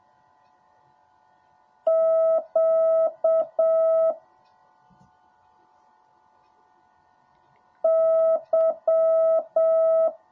Morse code, heard as a keyed tone, about 7 wpm, 650 Hz. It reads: Q Y
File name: jh8cbh / 2021cw / cw10a